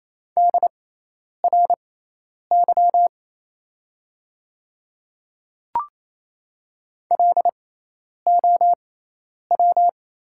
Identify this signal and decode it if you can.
Morse code, heard as a keyed tone, about 28 wpm, spaced as for 10 wpm, 700 Hz. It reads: DRY LOW
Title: 6x Character Spacing - Top 500 Words - 28wpm